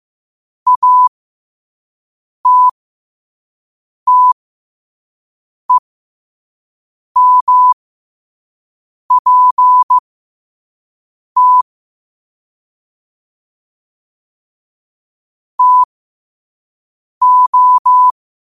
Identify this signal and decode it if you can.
Morse code, heard as a keyed tone, about 15 wpm, spaced as for 5 wpm, 1000 Hz. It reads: ATTEMPT TO
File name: 5wpm